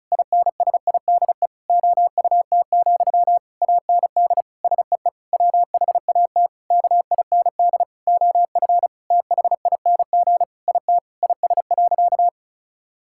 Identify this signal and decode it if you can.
Morse code, heard as a keyed tone, about 36 wpm, 700 Hz. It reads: INSIDE OUT, AND SEE WHAT KIND OF THING IT IS.
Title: Sentences from Top 1000 Words - Review Part 6 - 35wpm